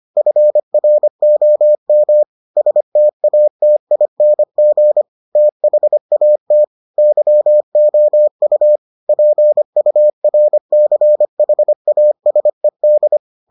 Morse code, heard as a keyed tone, about 25 wpm, 600 Hz. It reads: FROM STATING THAT YOU PURCHASED